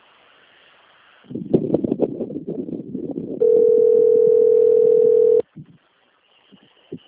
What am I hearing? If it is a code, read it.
Ringback tone (North America), 440 + 480 Hz.
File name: rg-1600-unknown- - - .35620